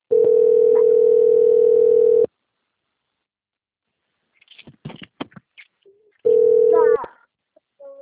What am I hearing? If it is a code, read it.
Ringback tone (North America), 440 + 480 Hz.